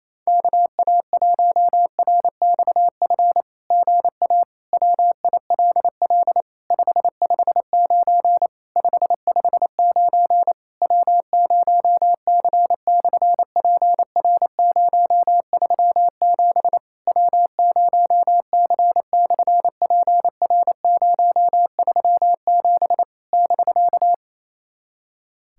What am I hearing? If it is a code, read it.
Morse code, heard as a keyed tone, about 28 wpm, 660 Hz. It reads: KA1RXF GA WILL 559 559 W0C/PR037 W0C/PR037 <BK>
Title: SOTA QSO Practice 28wpm